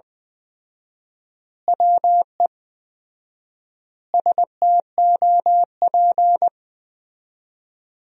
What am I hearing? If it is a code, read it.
Morse code, heard as a keyed tone, about 21 wpm, 700 Hz. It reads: WE STOP
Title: 4x Word-Spacing - Sentences from Top 300 Words - 20wpm